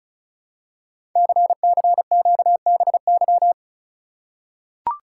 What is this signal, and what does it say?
Morse code, heard as a keyed tone, about 33 wpm, 700 Hz. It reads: CCQBY
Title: Code Groups - 5 Letters - 35wpm